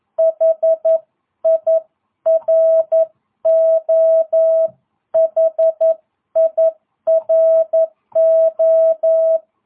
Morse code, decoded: HIRO HIRO

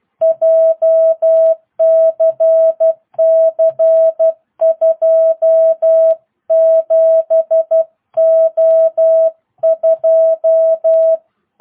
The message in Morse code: JCC 27O2

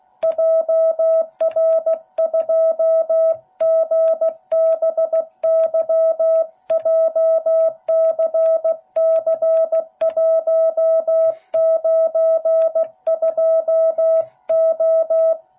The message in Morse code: JR2GBY JCC192O

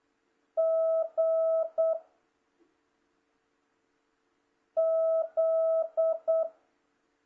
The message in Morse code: G Z

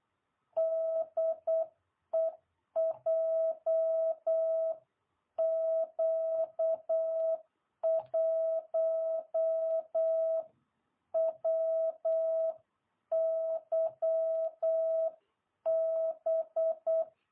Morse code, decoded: DE JQ1WYB

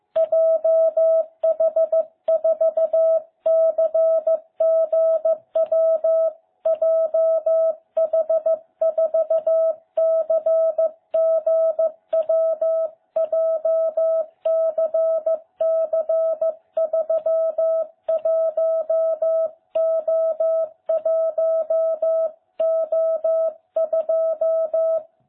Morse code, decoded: JH4CGW JH4CGW JCC31O1O2

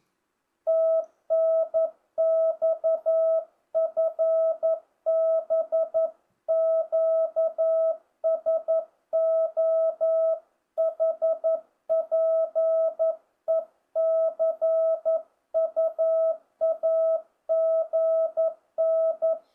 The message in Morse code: TNX FB QSO HPE CU AGN

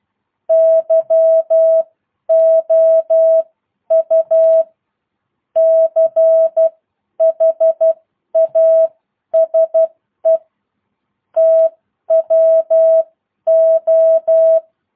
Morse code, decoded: YOU CHASE TWO

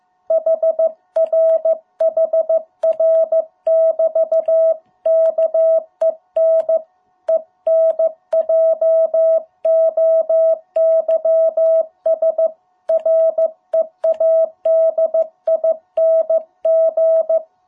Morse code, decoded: HR HR <BT> KEN ENJOYS READING

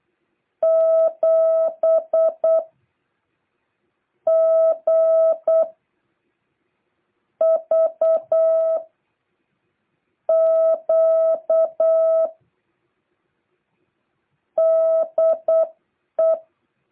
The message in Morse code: 7GVQ DE

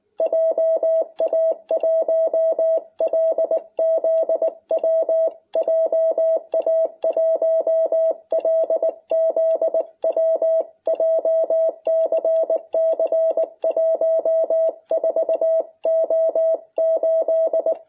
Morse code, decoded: JA1LZW JA1LZW JCC 14O8

